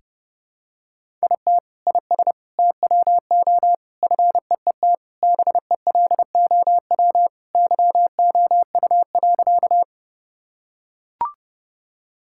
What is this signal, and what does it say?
Morse code, heard as a keyed tone, about 30 wpm, 700 Hz. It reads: IT IS TWO FEET BELOW YOU.
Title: Sentences from Top 1000 Words - Review Part 5 - 30wpm